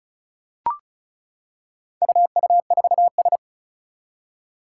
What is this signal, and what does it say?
Morse code, heard as a keyed tone, about 36 wpm, 700 Hz. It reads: UU4S
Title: Four Challenging Letter-Number Combinations 35wpm